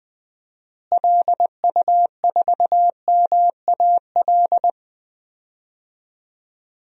Morse code, heard as a keyed tone, about 21 wpm, 700 Hz. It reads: LU4MAL
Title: DXCC-100 Call Signs - 6 Characters - Repeated 3x - 20wpm